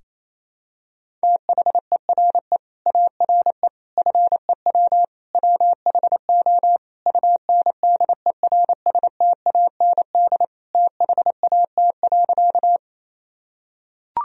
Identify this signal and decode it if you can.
Morse code, heard as a keyed tone, about 28 wpm, 700 Hz. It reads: THERE ARE FEW WHO UNDERSTAND THAT.
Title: Sentences from Top 1000 Words - Review Part 1 - 28wpm